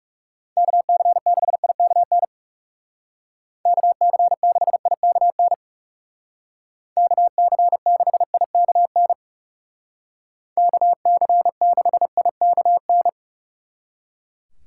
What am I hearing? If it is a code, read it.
Morse code, heard as a keyed tone, about 35 wpm, 690 Hz. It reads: KC6IKN KC6IKN KC6IKN KC6IKN
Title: Speed Racing - USA Call Signs 30wpm